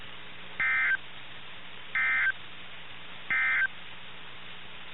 Emergency Alert System (SAME) tones, heard as harsh buzzing bursts. It is heard from 0.5 to 3.8 s.
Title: NOAA Weather Radio - KIH61